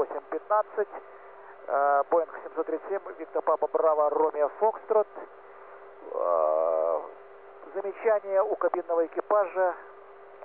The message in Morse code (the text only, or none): none